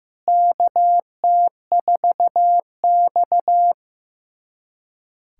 Morse code, heard as a keyed tone, about 15 wpm, 700 Hz. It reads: KT4X
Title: USA Callsign DE Callsign - 4 Characters - 15wpm